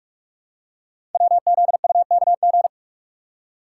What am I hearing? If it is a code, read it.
Morse code, heard as a keyed tone, about 44 wpm, 690 Hz. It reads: WZUKG
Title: Code Groups - 5 Letters - 45wpm